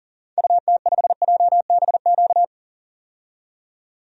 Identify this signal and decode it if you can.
Morse code, heard as a keyed tone, about 40 wpm, 700 Hz. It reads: UT5JBQ
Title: DXCC-100 Call Signs - 6 Characters - 40wpm